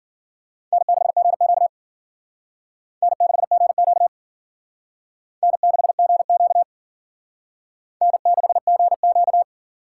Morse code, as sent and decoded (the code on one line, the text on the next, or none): -. -.... --. --.- / -. -.... --. --.- / -. -.... --. --.- / -. -.... --. --.-
N6GQ N6GQ N6GQ N6GQ